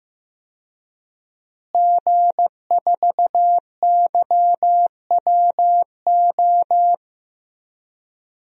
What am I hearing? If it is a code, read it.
Morse code, heard as a keyed tone, about 15 wpm, 700 Hz. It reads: G4YWO
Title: DXCC-100 Call Signs - 5 Characters - Repeated 3x - 15wpm